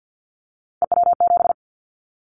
Morse code, heard as a keyed tone, about 50 wpm, 700 Hz. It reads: EW7
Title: Callsign Prefixes for Top 50 DXCC - 50wpm